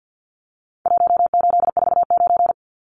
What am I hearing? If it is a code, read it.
Morse code, heard as a keyed tone, about 49 wpm, 700 Hz. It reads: 1849
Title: Numbers - Contest Style - 4-Digit 50wpm